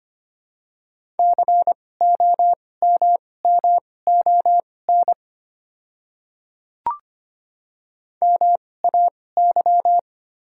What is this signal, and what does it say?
Morse code, heard as a keyed tone, about 25 wpm, spaced as for 18 wpm, 700 Hz. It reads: COMMON MAY
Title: 2x Character Spacing - Top 500 Words - 25wpm